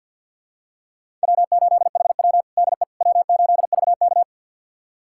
Morse code, heard as a keyed tone, about 45 wpm, 700 Hz. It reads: W8HW DE W8VK
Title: USA Callsign DE Callsign - 4 Characters - 50wpm